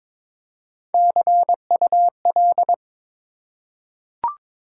Morse code, heard as a keyed tone, about 22 wpm, 700 Hz. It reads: CUL
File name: Top 90 QSO Elements - 3x Character Spacing - 22wpm